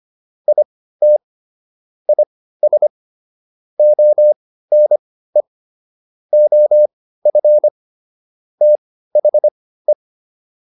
Morse code, heard as a keyed tone, about 25 wpm, spaced as for 15 wpm, 600 Hz. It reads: IT IS ONE OF THE